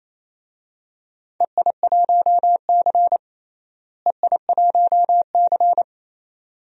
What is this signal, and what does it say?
Morse code, heard as a keyed tone, about 28 wpm, 700 Hz. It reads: EI1C EI1C
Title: DXCC-100 Call Signs - 4 Characters - Repeated 2x - 28wpm